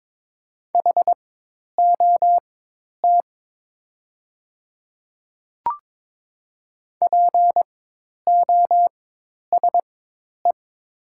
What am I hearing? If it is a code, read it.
Morse code, heard as a keyed tone, about 22 wpm, spaced as for 10 wpm, 700 Hz. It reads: HOT POSE
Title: 4x Character Spacing - Top 500 Words - 22wpm